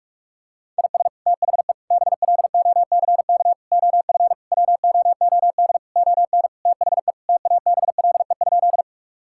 Morse code, heard as a keyed tone, about 46 wpm, 700 Hz. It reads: IS THE BLOCK OF WOOD ON THE TABLE?
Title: Sentences from Top 1600 Words 45wpm